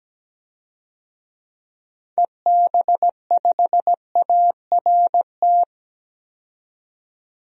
Morse code, heard as a keyed tone, about 17 wpm, 700 Hz. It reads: EB5ART
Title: DXCC-100 Call Signs - 6 Characters - 17wpm